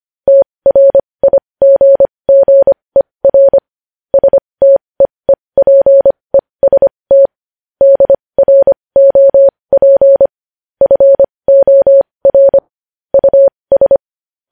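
Morse code, heard as a keyed tone, about 25 wpm, spaced as for 20 wpm, 550 Hz. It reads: TRIGGER STEEPEST DROP FOR US
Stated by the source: Morse code news headlines practice at 20 wpm.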